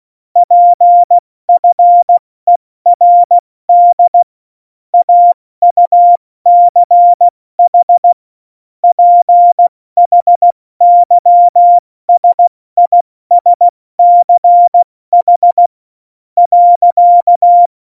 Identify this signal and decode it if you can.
Morse code, heard as a keyed tone, about 16 wpm, spaced as for 14 wpm, 700 Hz. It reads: PFERD AUCH PHYSISCH.
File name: Morsekurs / Witze / Witz 05 Mai, 2020 70 F80